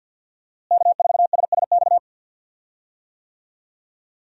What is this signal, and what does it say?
Morse code, heard as a keyed tone, about 47 wpm, 700 Hz. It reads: K4SSX
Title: USA Call Signs - 5 Characters - Repeated 3x - 50wpm